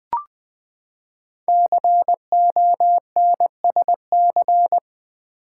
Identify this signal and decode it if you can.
Morse code, heard as a keyed tone, about 20 wpm, 700 Hz. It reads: CONSC
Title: Build A Word - 5 Letters At A Time - 10-Letter Words - 20wpm